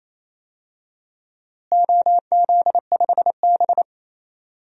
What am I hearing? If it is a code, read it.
Morse code, heard as a keyed tone, about 28 wpm, 700 Hz. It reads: OZ5B
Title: DXCC-100 Call Signs - 4 Characters - Repeated 3x - 28wpm